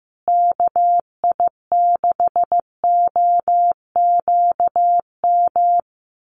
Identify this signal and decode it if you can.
Morse code, heard as a keyed tone, about 15 wpm, 700 Hz. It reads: KI6OQM
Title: USA Call Signs Repeated 3x 15wpm